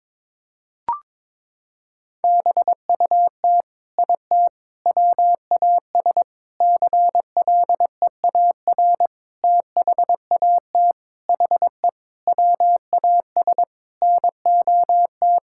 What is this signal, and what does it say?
Morse code, heard as a keyed tone, about 22 wpm, 700 Hz. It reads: BUT IT WAS CLEAR THAT HE WAS NOT
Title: Sentences from Top 1600 Words 22wpm